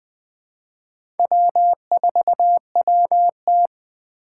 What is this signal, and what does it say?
Morse code, heard as a keyed tone, about 21 wpm, 700 Hz. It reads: W4WT
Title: USA Call Signs - 4x Character Spacing - 20wpm